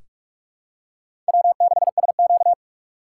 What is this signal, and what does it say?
Morse code, heard as a keyed tone, about 46 wpm, 700 Hz. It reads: W6SQ
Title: USA Call Signs - 4 Characters - 2x Character Spacing - 45wpm